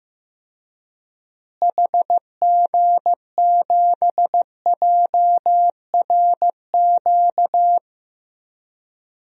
Morse code, heard as a keyed tone, about 15 wpm, 700 Hz. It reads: HG7JRQ
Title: DXCC-100 Call Signs - 6 Characters - Repeated 2x - 15wpm